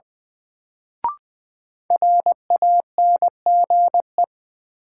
Morse code, heard as a keyed tone, about 20 wpm, 700 Hz. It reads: RANGE
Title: Sets of 4 Words - New Word Drill - 20wpm